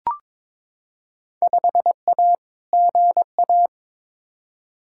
Morse code, heard as a keyed tone, about 22 wpm, 700 Hz. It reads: 5A GA
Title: Field Day Exchanges Encore - 22wpm